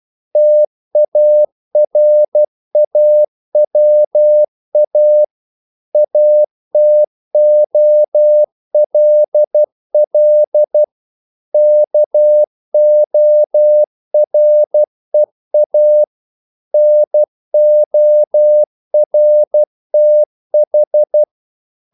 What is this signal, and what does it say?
Morse code, heard as a keyed tone, about 12 wpm, 600 Hz. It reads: TARAWA ATOLL KOREA NORTH
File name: CW / Practice / NationalCapitals12